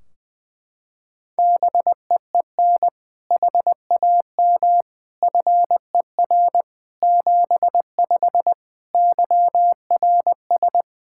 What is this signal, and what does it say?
Morse code, heard as a keyed tone, about 20 wpm, 700 Hz. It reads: BEEN HAM FER 75 YRS
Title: Practice QSO - Exchange 3 - More Info - 20wpm